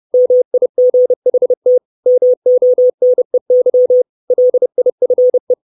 Morse code, heard as a keyed tone, about 30 wpm, 500 Hz. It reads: MIGHT MONEY LIFE
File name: 632 Words 30WPM